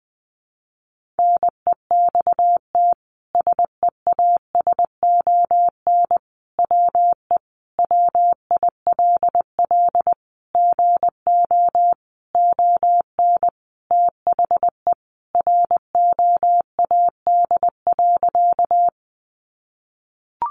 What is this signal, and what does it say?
Morse code, heard as a keyed tone, about 20 wpm, 700 Hz. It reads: NEXT SEASON WE WILL GO ON THE ROAD.
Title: Sentences from Top 1400 Words 20wpm